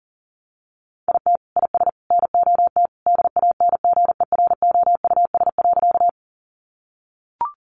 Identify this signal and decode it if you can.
Morse code, heard as a keyed tone, about 40 wpm, 700 Hz. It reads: IT IS NOT DANGEROUS.
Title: Sentences from Top 1200 Words 40wpm